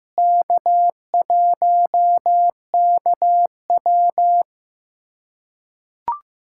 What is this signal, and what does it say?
Morse code, heard as a keyed tone, about 15 wpm, 700 Hz. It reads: K1KW
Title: USA Callsign DE Callsign - 4 Characters - 15wpm